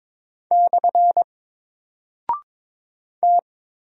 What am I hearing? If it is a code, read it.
Morse code, heard as a keyed tone, about 22 wpm, 700 Hz. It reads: / T
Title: Single Character (Letter:Number:Punctuation:Prosign) 22wpm